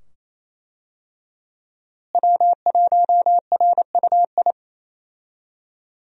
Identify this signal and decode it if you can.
Morse code, heard as a keyed tone, about 28 wpm, 700 Hz. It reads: W1RUI
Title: USA Call Signs - 5 Characters - Repeated 2x - 28wpm